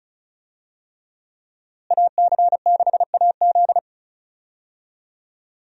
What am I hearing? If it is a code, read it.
Morse code, heard as a keyed tone, about 35 wpm, 700 Hz. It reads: AC6AZ
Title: USA Call Signs - 5 Characters - Repeated 3x - 35wpm